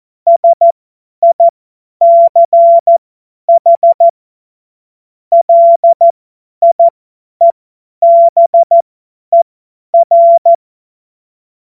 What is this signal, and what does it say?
Morse code, heard as a keyed tone, about 14 wpm, spaced as for 10 wpm, 680 Hz. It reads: SICH LIEBER
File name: Morsekurs / Witze / Witz 15. Februar 2022 50 F70